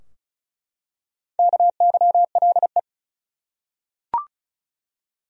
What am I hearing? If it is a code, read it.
Morse code, heard as a keyed tone, about 34 wpm, 700 Hz. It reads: KYLE
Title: USA Common Names - Max 4 Letters - 35wpm